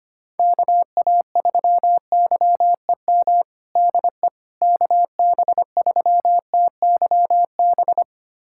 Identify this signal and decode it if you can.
Morse code, heard as a keyed tone, about 25 wpm, 700 Hz. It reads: KA3YEM DE KB3TYB
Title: USA Callsign DE Callsign 25wpm